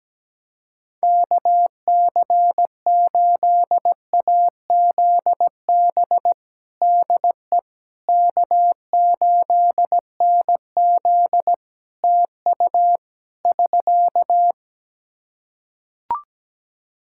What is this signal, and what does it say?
Morse code, heard as a keyed tone, about 17 wpm, 700 Hz. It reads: KC8AZB DE K8NZ TU <SK>
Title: Practice QSO - Exchange 4 - Ending - 17wpm